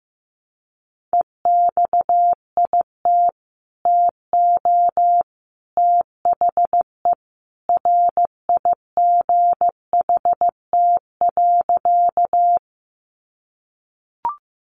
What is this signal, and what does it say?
Morse code, heard as a keyed tone, about 15 wpm, 700 Hz. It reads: EXIT TO THE RIGHT.
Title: Sentences from Top 1100 Words 15wpm